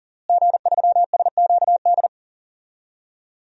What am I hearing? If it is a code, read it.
Morse code, heard as a keyed tone, about 37 wpm, 700 Hz. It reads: G3SQD
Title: DXCC-100 Call Signs - 5 Characters - Repeated 3x - 40wpm